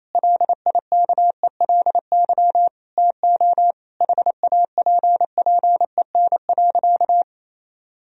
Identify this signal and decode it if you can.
Morse code, heard as a keyed tone, about 28 wpm, 700 Hz. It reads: LIKELY TO HAPPEN.
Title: Sentences from Top 1000 Words - Review Part 4 - 28wpm